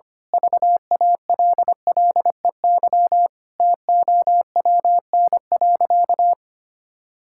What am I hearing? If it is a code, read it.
Morse code, heard as a keyed tone, about 25 wpm, 700 Hz. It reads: VALLEY TOWN.
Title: Pangram Challenge v2 - 25wpm